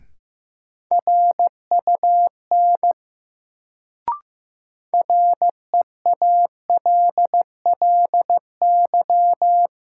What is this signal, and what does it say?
Morse code, heard as a keyed tone, about 15 wpm, 700 Hz. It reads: RUN REALLY
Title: Sets of 4 Words - New Word Drill - 15wpm